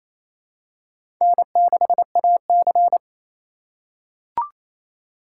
Morse code, heard as a keyed tone, about 27 wpm, 690 Hz. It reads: N6AC
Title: USA Call Signs - 4 Characters - 2x Character Spacing - 28wpm